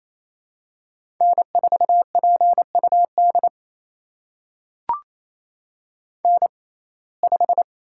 Morse code, heard as a keyed tone, about 28 wpm, spaced as for 20 wpm, 700 Hz. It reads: N4PUD N5
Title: USA Call Signs - 5 Characters - 6x Character Spacing - 28wpm